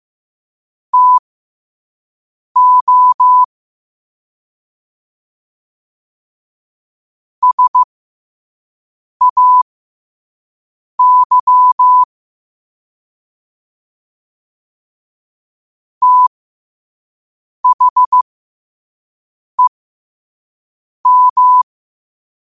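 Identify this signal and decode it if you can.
Morse code, heard as a keyed tone, about 15 wpm, spaced as for 5 wpm, 1000 Hz. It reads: TO SAY THEM
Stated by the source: Morse code quote of the day sent at 5 WPM.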